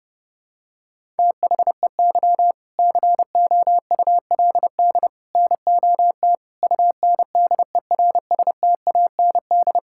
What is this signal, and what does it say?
Morse code, heard as a keyed tone, about 30 wpm, 700 Hz. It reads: THEY COULD NOT UNDERSTAND
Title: Sentences from Top 1000 Words - Review Part 6 - 30wpm